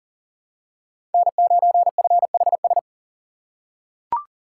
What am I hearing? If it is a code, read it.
Morse code, heard as a keyed tone, about 40 wpm, 700 Hz. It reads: N9FHS
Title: USA Call Signs - 5 Characters - 2x Character Spacing - 40wpm